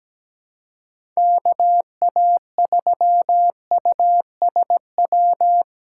Morse code, heard as a keyed tone, about 17 wpm, 700 Hz. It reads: KA3USW
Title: USA Call Signs - 6 Characters - 6x Character Spacing - 17wpm